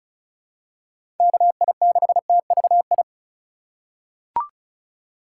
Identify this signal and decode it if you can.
Morse code, heard as a keyed tone, about 36 wpm, 700 Hz. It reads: KI6TVI
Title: USA Call Signs - 6 Characters - 3x Character Spacing - 35wpm